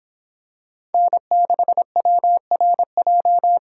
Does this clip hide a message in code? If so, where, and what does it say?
Yes: Morse code from 0.9 to 3.6 s, heard as a keyed tone, about 25 wpm, 700 Hz. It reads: N6WRJ